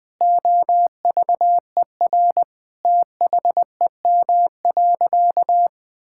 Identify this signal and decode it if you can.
Morse code, heard as a keyed tone, about 20 wpm, 700 Hz. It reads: OVER THEM.
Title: Sentences from Top 1000 Words - Review Part 6 - 20wpm